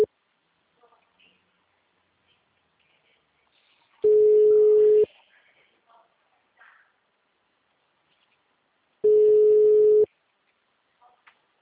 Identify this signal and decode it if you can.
Ringback tone (Europe, 425 Hz).
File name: out- -2003- - - .48205